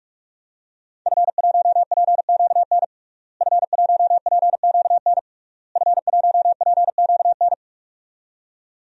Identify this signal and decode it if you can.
Morse code, heard as a keyed tone, about 46 wpm, 700 Hz. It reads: F1PQN F1PQN F1PQN
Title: DXCC-100 Call Signs - 5 Characters - Repeated 3x - 45wpm